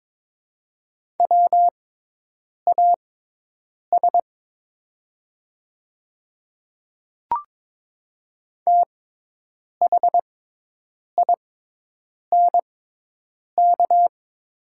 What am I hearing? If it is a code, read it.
Morse code, heard as a keyed tone, about 22 wpm, spaced as for 8 wpm, 700 Hz. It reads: WAS THINK
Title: 6x Character Spacing - Top 200 Words - 22wpm